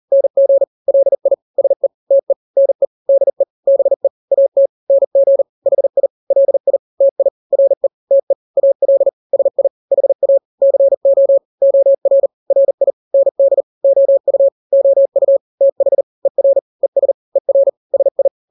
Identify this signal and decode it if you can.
Morse code, heard as a keyed tone, about 40 wpm, 550 Hz. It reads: NG LI SE TE NE DE BE AT NG HI LI TI RE TE AL SI HA CO OR RI ND OU OU TH ER ES ER SI
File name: n-grams / 2-grams 40 35 WPM